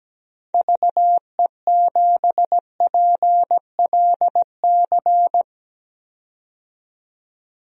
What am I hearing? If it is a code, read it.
Morse code, heard as a keyed tone, about 17 wpm, 700 Hz. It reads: VE7PLC
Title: DXCC-100 Call Signs - 6 Characters - 17wpm